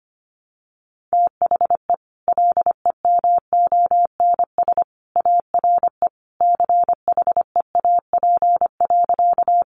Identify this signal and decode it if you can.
Morse code, heard as a keyed tone, about 25 wpm, 700 Hz. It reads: THE LEMONS ARE CHEAP.
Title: Additional Easy Phrases 25wpm